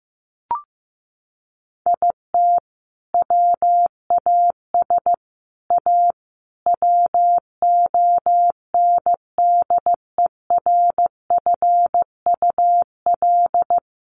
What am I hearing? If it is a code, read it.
Morse code, heard as a keyed tone, about 15 wpm, 700 Hz. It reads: IT WAS A WONDERFUL
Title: Sentences from Top 1500 Words 15wpm